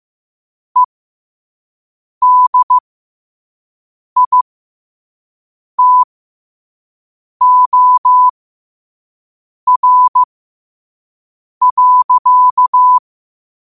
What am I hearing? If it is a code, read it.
Morse code, heard as a keyed tone, about 15 wpm, spaced as for 5 wpm, 1000 Hz. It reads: EDITOR.
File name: qotdfiles / 5wpm